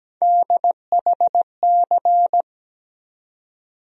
Morse code, heard as a keyed tone, about 17 wpm, 700 Hz. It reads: DHC
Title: USA Call Sign Postfixes - 3 Letters 17wpm